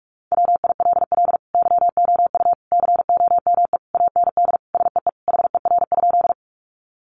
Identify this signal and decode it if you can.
Morse code, heard as a keyed tone, about 44 wpm, 700 Hz. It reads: WILL YOU COME AND SEE HER?
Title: Sentences from Top 100 Words 45wpm